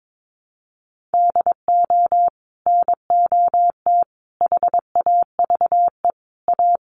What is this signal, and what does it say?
Morse code, heard as a keyed tone, about 22 wpm, 700 Hz. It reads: DO NOT HAVE A
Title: Sets of 5 Words Encore 22wpm